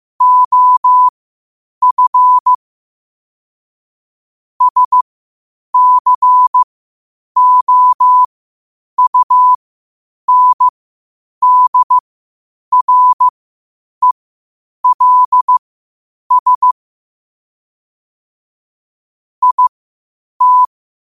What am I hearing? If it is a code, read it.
Morse code, heard as a keyed tone, about 15 wpm, spaced as for 8 wpm, 1000 Hz. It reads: OF SCOUNDRELS IT
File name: qotdfiles / 8wpm